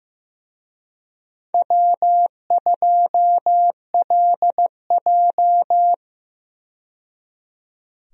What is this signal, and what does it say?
Morse code, heard as a keyed tone, about 15 wpm, 700 Hz. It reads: W2LJ
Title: USA Call Signs - 4 Characters - 15wpm